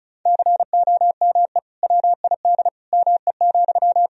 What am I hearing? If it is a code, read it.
Morse code, heard as a keyed tone, about 34 wpm, 700 Hz. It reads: COME WID ME,
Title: Sentences from Top 200 Words and Abbreviations 35wpm